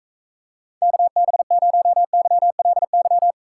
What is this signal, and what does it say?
Morse code, heard as a keyed tone, about 45 wpm, 700 Hz. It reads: KB0YLY